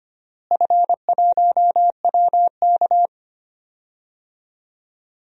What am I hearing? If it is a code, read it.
Morse code, heard as a keyed tone, about 25 wpm, 700 Hz. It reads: F1WK
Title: DXCC-100 Call Signs - 4 Characters - Repeated 3x - 25wpm